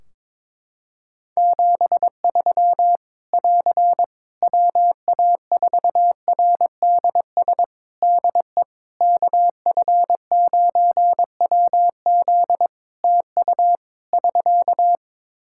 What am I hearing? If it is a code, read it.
Morse code, heard as a keyed tone, about 22 wpm, 700 Hz. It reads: 73 <AR> WA4RDS DE KF9WZ TU <SK>
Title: K1USN SST QSO Practice 22wpm